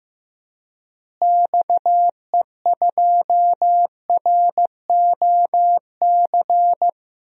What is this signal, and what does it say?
Morse code, heard as a keyed tone, about 15 wpm, 700 Hz. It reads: XE2ROC
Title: DXCC-100 Call Signs - 6 Characters - Repeated 2x - 15wpm